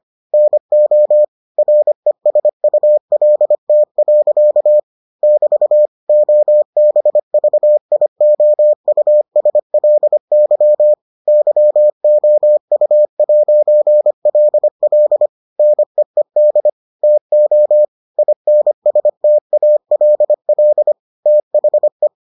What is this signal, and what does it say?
Morse code, heard as a keyed tone, about 25 wpm, 600 Hz. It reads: NO RESULT. <BT> OBVIOUSLY YOU'LL NEED TO INSTALL THE